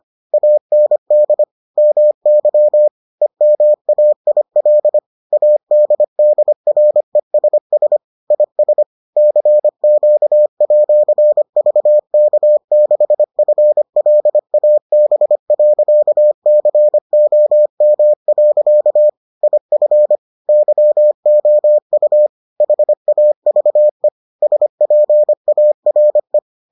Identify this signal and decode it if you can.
Morse code, heard as a keyed tone, about 25 wpm, 600 Hz. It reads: AND MY EMAIL ADDRESS IS CQ@VK6FLAB.COM. IF YOU HAVE SPARE